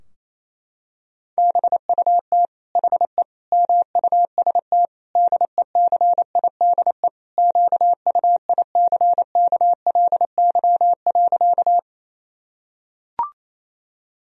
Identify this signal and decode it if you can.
Morse code, heard as a keyed tone, about 28 wpm, 700 Hz. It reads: BUT HE MUST DECIDE QUICKLY.
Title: Sentences from Top 1000 Words - Review Part 3 - 28wpm